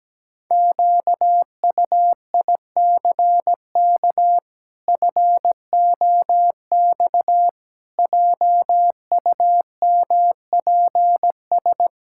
Morse code, heard as a keyed tone, about 17 wpm, 700 Hz. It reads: QUICK FOX JUMPS
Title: Pangram Challenge v2 - 17wpm